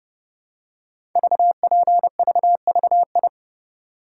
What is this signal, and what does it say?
Morse code, heard as a keyed tone, about 30 wpm, 700 Hz. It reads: VPVVI
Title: Code Groups - 5 Letters - 30wpm